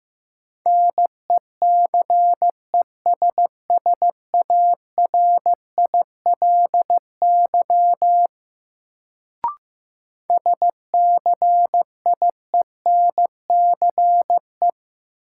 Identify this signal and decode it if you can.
Morse code, heard as a keyed tone, about 15 wpm, 700 Hz. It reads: NECESSARILY SCIENCE
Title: Sentences from Top 1400 Words - New Word Drill Alt - 15wpm